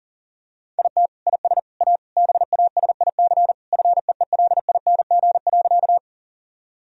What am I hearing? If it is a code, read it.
Morse code, heard as a keyed tone, about 37 wpm, 700 Hz. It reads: IT IS A BASIC FEELING.
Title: Sentences from Top 1000 Words - Review Part 3 - 40wpm